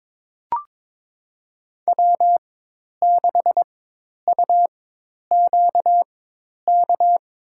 Morse code, heard as a keyed tone, about 23 wpm, spaced as for 10 wpm, 700 Hz. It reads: W6UQK
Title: USA Call Signs - 5 Characters - 4x Character Spacing - 22wpm